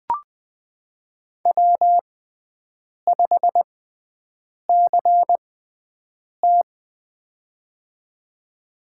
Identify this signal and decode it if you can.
Morse code, heard as a keyed tone, about 20 wpm, spaced as for 7 wpm, 700 Hz. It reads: W5CT
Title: USA Call Signs - 4 Characters - 6x Character Spacing - 20wpm